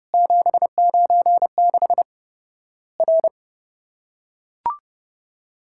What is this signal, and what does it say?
Morse code, heard as a keyed tone, about 30 wpm, 700 Hz. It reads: 796 R
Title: Worked All Europe DX Contest - QTC Exchange - 30wpm